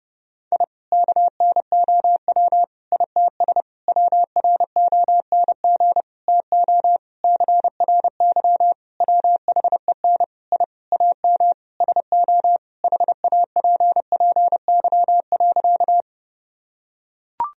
Morse code, heard as a keyed tone, about 30 wpm, 700 Hz. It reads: I KNOW ITS WRONG TO CRY WHEN I AM SO HAPPY.
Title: Sentences from Top 1000 Words - Review Part 2 - 30wpm